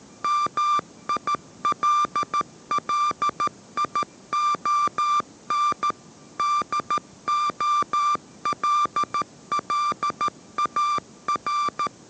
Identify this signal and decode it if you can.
Morse code, heard as a keyed tone, about 13 wpm, 1180 Hz. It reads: MILLION DOLLAR